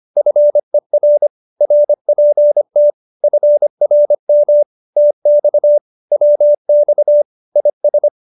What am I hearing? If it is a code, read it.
Morse code, heard as a keyed tone, about 25 wpm, 590 Hz. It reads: FER RPT FRM TX WX IS
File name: qso201b 25